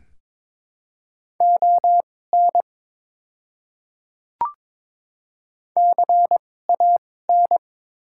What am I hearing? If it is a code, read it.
Morse code, heard as a keyed tone, about 22 wpm, spaced as for 16 wpm, 700 Hz. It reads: ON CAN
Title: 2x Character Spacing - Top 500 Words - 22wpm